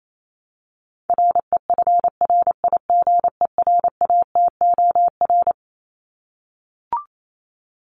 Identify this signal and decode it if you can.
Morse code, heard as a keyed tone, about 28 wpm, 700 Hz. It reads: REFRIGERATOR
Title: Top 1501-1750 Words 28wpm